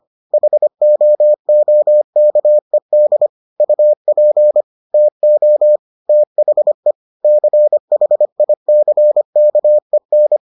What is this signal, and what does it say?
Morse code, heard as a keyed tone, about 25 wpm, 600 Hz. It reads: HOOKED UP TO THE CHICKEN